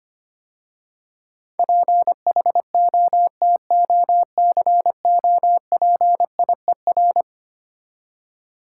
Morse code, heard as a keyed tone, about 26 wpm, 700 Hz. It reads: PHOTOCOPIER
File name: practice / 25wpm / Difficult Words - Speed-Racing with Hints - 25wpm